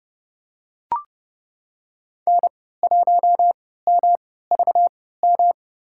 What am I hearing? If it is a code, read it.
Morse code, heard as a keyed tone, about 32 wpm, spaced as for 17 wpm, 690 Hz. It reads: N1MVM
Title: USA Call Signs - 3x Character Spacing - 30wpm